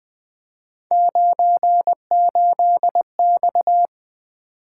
Morse code, heard as a keyed tone, about 20 wpm, 700 Hz. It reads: 98X
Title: Three Letter-Number Combinations - 20wpm